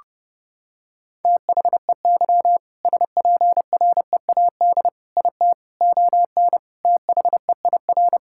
Morse code, heard as a keyed tone, about 30 wpm, 700 Hz. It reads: THEY SPREAD IT ON THEIR
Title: Sentences from Top 1600 Words 30wpm